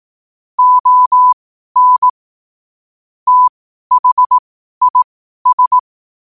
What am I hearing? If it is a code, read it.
Morse code, heard as a keyed tone, about 18 wpm, spaced as for 12 wpm, 1000 Hz. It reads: ON THIS
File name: qotdfiles / 12wpm